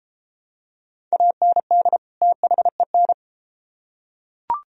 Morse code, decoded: AND THEN

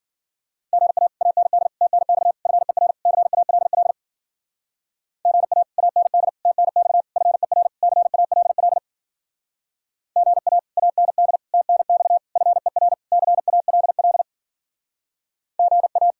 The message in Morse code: GA AND TNX FER CALL GA AND TNX FER CALL GA AND TNX FER CALL GA

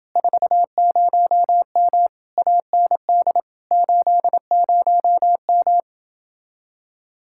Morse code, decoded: 40M AND 80M

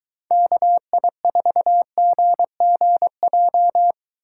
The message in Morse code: KI4GGJ

